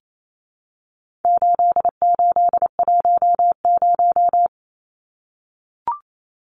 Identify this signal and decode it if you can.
Morse code, heard as a keyed tone, about 28 wpm, 700 Hz. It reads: 8810
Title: Numbers - 4-Digit 28wpm